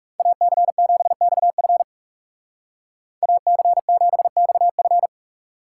Morse code, heard as a keyed tone, about 36 wpm, 700 Hz. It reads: AC7XF AC7XF
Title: Speed Racing - USA Call Signs 30wpm